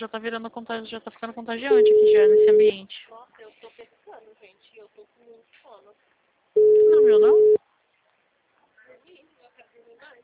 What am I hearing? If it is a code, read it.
Ringback tone (Europe, 425 Hz), with speech.